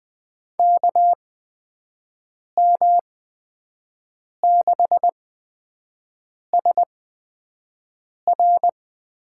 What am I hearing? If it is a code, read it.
Morse code, heard as a keyed tone, about 20 wpm, spaced as for 5 wpm, 700 Hz. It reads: KM6SR
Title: USA Call Signs - 5 Characters - 8x Character Spacing - 20wpm